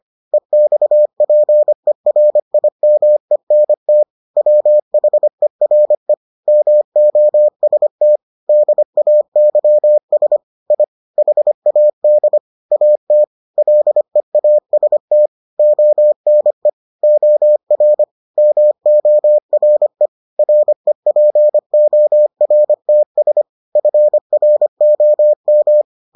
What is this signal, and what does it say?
Morse code, heard as a keyed tone, about 25 wpm, 600 Hz. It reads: EXPERIMENT WHERE MOST DAYS I HAD AT LEAST ONE OR MORE REPORTS FROM